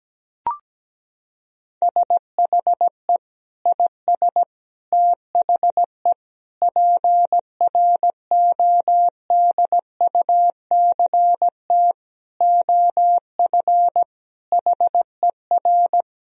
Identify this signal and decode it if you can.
Morse code, heard as a keyed tone, about 17 wpm, 700 Hz. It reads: SHE IS THE PRODUCT OF HER
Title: Sentences from Top 1000 Words - Review Part 2 - 17wpm